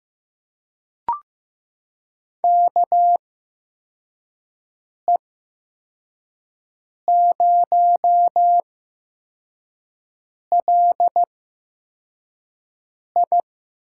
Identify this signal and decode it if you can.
Morse code, heard as a keyed tone, about 15 wpm, spaced as for 4 wpm, 700 Hz. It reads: KE0LI
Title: USA Call Signs - 6 Characters - 8x Character Spacing - 15wpm